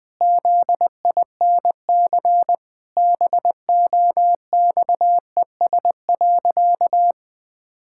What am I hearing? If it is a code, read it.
Morse code, heard as a keyed tone, about 21 wpm, 700 Hz. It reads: ZINC BOXES.
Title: Pangram Challenge v2 - 20wpm